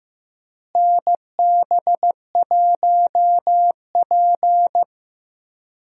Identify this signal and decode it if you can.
Morse code, heard as a keyed tone, about 16 wpm, 700 Hz. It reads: NB1P
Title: USA Call Signs - 4 Characters - Repeated 3x - 15wpm